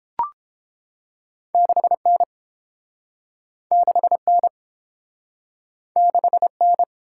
Morse code, heard as a keyed tone, about 30 wpm, 700 Hz. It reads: 6N 6N 6N